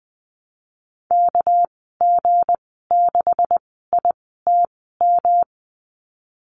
Morse code, heard as a keyed tone, about 20 wpm, spaced as for 15 wpm, 700 Hz. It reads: KG6ITM
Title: 219 Lesson 35g - Number 0 - Call Signs 20wpm